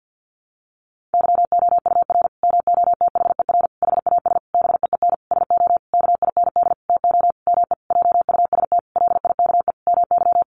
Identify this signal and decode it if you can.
Morse code, heard as a keyed tone, about 50 wpm, 700 Hz. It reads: YOUR MOTHER HAS BEEN SO KIND TO ME JUST LIKE MY
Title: Sentences from Top 1200 Words 50wpm